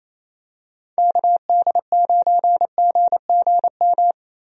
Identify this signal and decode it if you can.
Morse code, heard as a keyed tone, about 28 wpm, 700 Hz. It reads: KD9GGM
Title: USA Call Signs - 6 Characters - Repeated 3x - 28wpm